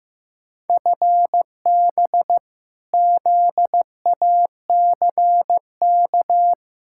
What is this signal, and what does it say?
Morse code, heard as a keyed tone, about 15 wpm, 700 Hz. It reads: FB ZACK
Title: Practice QSO - Exchange 3 - More Info - 15wpm